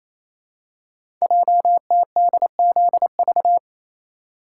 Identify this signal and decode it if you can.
Morse code, heard as a keyed tone, about 28 wpm, 700 Hz. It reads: JTDZV
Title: Code Groups - 5 Letters - 28wpm